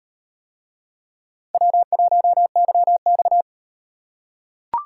Morse code, heard as a keyed tone, about 38 wpm, 700 Hz. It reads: W1YX